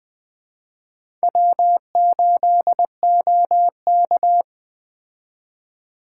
Morse code, heard as a keyed tone, about 20 wpm, 700 Hz. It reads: W8OK
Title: Speed Racing - USA Call Signs 17wpm